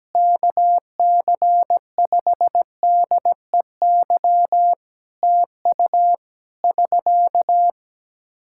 Morse code, heard as a keyed tone, about 17 wpm, 700 Hz. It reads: KC5DEY TU <SK>
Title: Practice QSO - Exchange 4 - Ending - 17wpm